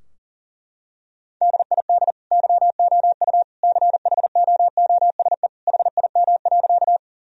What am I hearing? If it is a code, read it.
Morse code, heard as a keyed tone, about 37 wpm, 700 Hz. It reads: DID YOU CHOOSE HIM.
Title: Sentences from Top 1000 Words - Review Part 4 - 40wpm